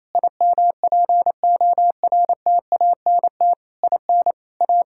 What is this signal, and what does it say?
Morse code, heard as a keyed tone, about 28 wpm, 700 Hz. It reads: IMPORTANT IN A
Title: Sentences from Top 1000 Words - Review Part 5 - 28wpm